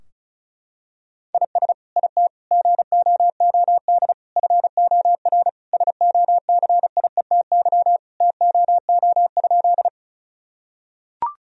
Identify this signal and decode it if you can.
Morse code, heard as a keyed tone, about 34 wpm, 700 Hz. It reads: IS IT GOOD FOR SOCIETY TOO?
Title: Sentences from Top 1000 Words - Review Part 4 - 35wpm